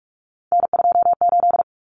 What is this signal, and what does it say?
Morse code, heard as a keyed tone, about 44 wpm, 700 Hz. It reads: N28
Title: Common Cut Numbers - 3-Digit 45wpm